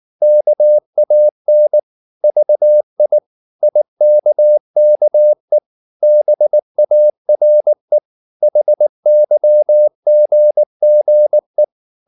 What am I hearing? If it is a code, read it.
Morse code, heard as a keyed tone, about 19 wpm, 600 Hz. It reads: KAN VI IKKE BARE HYGGE
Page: Ferie 19 wpm